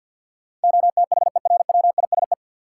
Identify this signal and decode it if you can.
Morse code, heard as a keyed tone, about 49 wpm, 700 Hz. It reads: OTHERWISE
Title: Sentences from Top 1200 Words - New Word Drill Alt - 50wpm